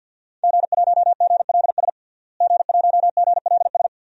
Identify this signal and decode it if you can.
Morse code, heard as a keyed tone, about 50 wpm, 700 Hz. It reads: G1GLS G1GLS
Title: DXCC-100 Call Signs - 5 Characters - Repeated 2x - 50wpm